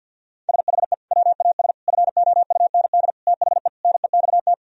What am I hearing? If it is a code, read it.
Morse code, heard as a keyed tone, about 50 wpm, 700 Hz. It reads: SHE WAS FOUND THE NEXT
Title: Sentences from Top 1000 Words - Review Part 3 - 50wpm